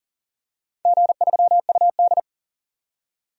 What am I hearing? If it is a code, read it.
Morse code, heard as a keyed tone, about 37 wpm, 700 Hz. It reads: G3UD
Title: DXCC-100 Call Signs - 4 Characters - 40wpm